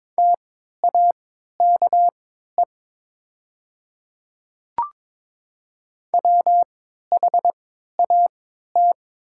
Morse code, heard as a keyed tone, about 22 wpm, spaced as for 12 wpm, 700 Hz. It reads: TAKE WHAT
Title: 3x Character Spacing - Top 300 Words - 22wpm